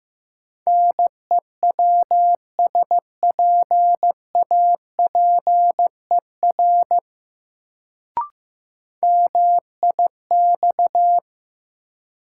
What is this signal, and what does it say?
Morse code, heard as a keyed tone, about 15 wpm, 700 Hz. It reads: NEWSPAPER MIX
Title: Sentences from Top 1000 Words - Review Part 4 - New Word Drill - 15wpm